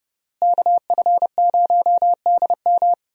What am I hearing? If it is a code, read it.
Morse code, heard as a keyed tone, about 29 wpm, 700 Hz. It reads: KF0DM
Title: USA Call Signs - 5 Characters - Repeated 2x - 30wpm